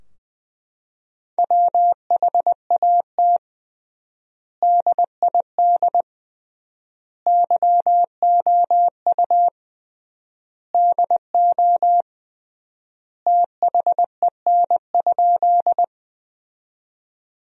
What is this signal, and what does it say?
Morse code, heard as a keyed tone, about 21 wpm, 700 Hz. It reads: WHAT DID YOU DO THEN?
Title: 3x Word-Spacing - Sentences from Top 100 Words - 20wpm